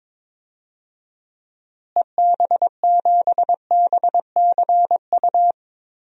Morse code, heard as a keyed tone, about 22 wpm, 700 Hz. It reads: EB7BCU
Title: DXCC-100 Call Signs - 6 Characters - Repeated 3x - 22wpm